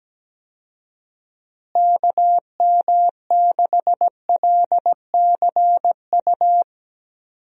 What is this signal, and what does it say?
Morse code, heard as a keyed tone, about 17 wpm, 700 Hz. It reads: KM6LCU
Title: USA Call Signs - 6 Characters - Repeated 3x - 17wpm